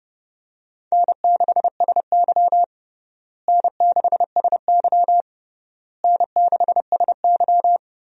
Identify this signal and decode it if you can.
Morse code, heard as a keyed tone, about 32 wpm, 700 Hz. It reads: N6SY N6SY N6SY
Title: USA Call Signs - 4 Characters - Repeated 3x - 30wpm